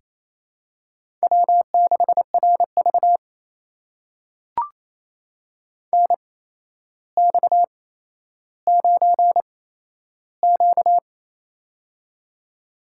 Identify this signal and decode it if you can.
Morse code, heard as a keyed tone, about 28 wpm, spaced as for 12 wpm, 700 Hz. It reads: W6RV NX9Q
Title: USA Call Signs - 4 Characters - 8x Character Spacing - 28wpm